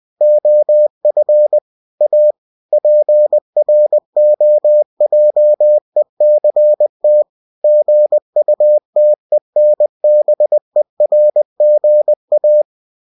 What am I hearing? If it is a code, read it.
Morse code, heard as a keyed tone, about 20 wpm, 600 Hz. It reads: OF A PROJECT GUTENBERGA
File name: cw-books / winnie-the-pooh / winnie-20wpm-page-0116